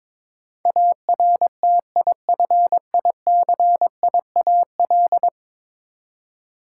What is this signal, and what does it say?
Morse code, heard as a keyed tone, about 22 wpm, 700 Hz. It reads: ARTIFICIAL
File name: practice / 22wpm / 10 Letter Words - First 3 Letters Given - 22wpm